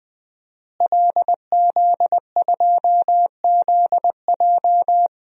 Morse code, heard as a keyed tone, about 20 wpm, 700 Hz. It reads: LZ2ZJ
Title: DXCC-100 Call Signs - 5 Characters - Repeated 2x - 20wpm